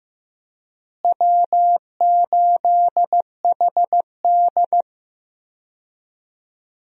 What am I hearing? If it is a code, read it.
Morse code, heard as a keyed tone, about 15 wpm, 700 Hz. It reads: W8HD
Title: USA Call Signs - 4 Characters - Repeated 3x - 15wpm